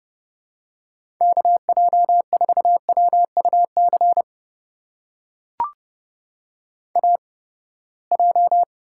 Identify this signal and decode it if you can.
Morse code, heard as a keyed tone, about 31 wpm, spaced as for 21 wpm, 700 Hz. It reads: KJ4WUC AJ
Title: USA Call Signs - 8x Character Spacing - 30wpm